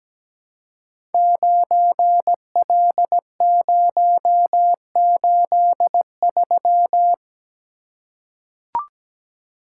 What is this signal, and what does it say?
Morse code, heard as a keyed tone, about 17 wpm, 700 Hz. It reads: 9L083
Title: Code Groups - 5 Letter:Numbers - 17wpm